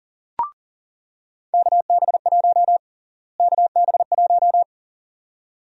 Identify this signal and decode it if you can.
Morse code, heard as a keyed tone, about 40 wpm, 700 Hz. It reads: KB1 KB1
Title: USA Call Sign Prefixes - Repeated 2x - 40wpm